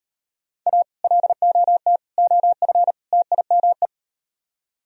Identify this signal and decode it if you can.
Morse code, heard as a keyed tone, about 38 wpm, 700 Hz. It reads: A LOT OF TIME